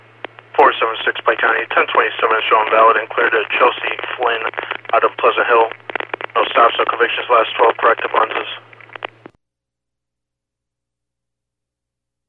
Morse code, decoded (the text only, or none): none